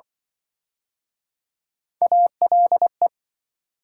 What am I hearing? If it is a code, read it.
Morse code, heard as a keyed tone, about 28 wpm, 700 Hz. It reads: ALE